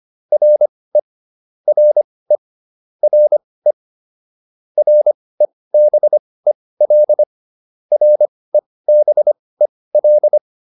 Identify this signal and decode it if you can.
Morse code, heard as a keyed tone, about 25 wpm, spaced as for 18 wpm, 600 Hz. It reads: RE RE RE REBEL REBEL